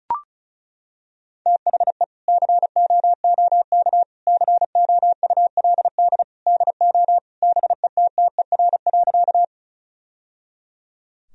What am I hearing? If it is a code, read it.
Morse code, heard as a keyed tone, about 36 wpm, 700 Hz. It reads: THE COOK COULD DO BETTER.
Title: Sentences from Top 1000 Words - Review Part 4 - 35wpm